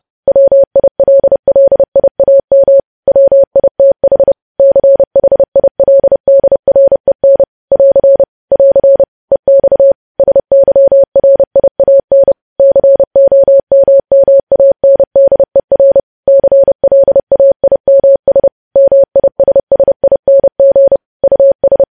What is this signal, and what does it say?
Morse code, heard as a keyed tone, about 30 wpm, 550 Hz. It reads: WILLIAM WITH CHILDREN <AR> <AR> EX SYRIAN COMMANDER CLAIMS MISSING US
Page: news headlines in morse code at 30 wpm